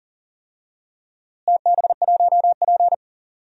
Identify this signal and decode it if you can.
Morse code, heard as a keyed tone, about 40 wpm, 700 Hz. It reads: TB1P
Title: DXCC-100 Call Signs - 4 Characters - Repeated 3x - 40wpm